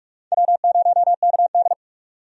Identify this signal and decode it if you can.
Morse code, heard as a keyed tone, about 46 wpm, 700 Hz. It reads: W0KD
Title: USA Call Signs - 4 Characters - 45wpm